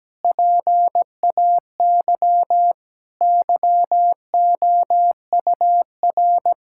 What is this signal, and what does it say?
Morse code, heard as a keyed tone, about 17 wpm, 700 Hz. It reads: PAY YOUR
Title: Sentences from Top 1000 Words - Review Part 5 - 17wpm